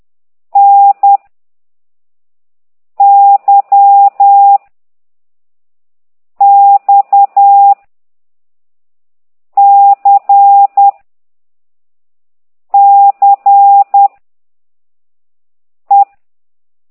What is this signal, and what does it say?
Morse code, heard as a keyed tone, about 10 wpm, 800 Hz. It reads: N Y X C C E
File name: ct2idw / exerc9